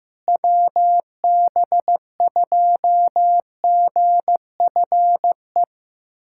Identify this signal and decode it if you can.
Morse code, heard as a keyed tone, about 15 wpm, 700 Hz. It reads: WB2GFE
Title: USA Callsign DE Callsign 15wpm